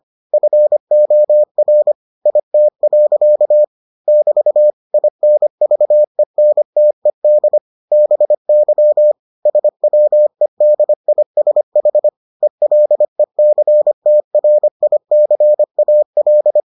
Morse code, heard as a keyed tone, about 25 wpm, 600 Hz. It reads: FOR IT. <BT> INVENTED BY SWEDISH ELECTRICAL